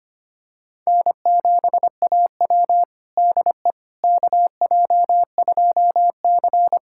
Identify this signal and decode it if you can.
Morse code, heard as a keyed tone, about 25 wpm, 700 Hz. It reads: N7AW DE KJ2C
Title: USA Callsign DE Callsign - 4 Characters - 25wpm